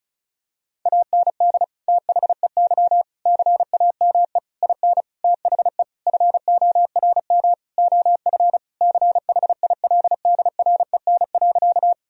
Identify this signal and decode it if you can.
Morse code, heard as a keyed tone, about 34 wpm, 700 Hz. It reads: AND THEY CAME IN THE FORM OF CHILDREN.
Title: Sentences from Top 1000 Words - Review Part 1 - 35wpm